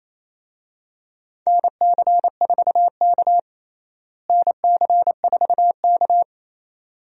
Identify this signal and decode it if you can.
Morse code, heard as a keyed tone, about 28 wpm, 700 Hz. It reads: NC4K NC4K
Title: USA Call Signs - 4 Characters - Repeated 2x - 28wpm